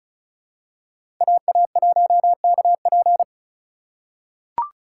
Morse code, heard as a keyed tone, about 34 wpm, 700 Hz. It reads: AA1KP
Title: USA Call Signs - 5 Characters - 4x Character Spacing - 35wpm